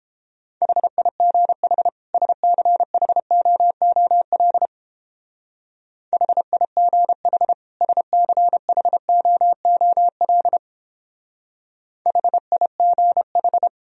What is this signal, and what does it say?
Morse code, heard as a keyed tone, about 29 wpm, 700 Hz. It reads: HIGH SCHOOL HIGH SCHOOL HIGH